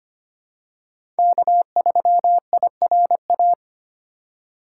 Morse code, heard as a keyed tone, about 25 wpm, 700 Hz. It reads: K3IRA
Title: USA Call Signs - 5 Characters - 3x Character Spacing - 25wpm